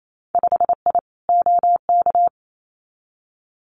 Morse code, heard as a keyed tone, about 28 wpm, 700 Hz. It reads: 5I OK
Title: Winter Field Day Exchanges Encore 28wpm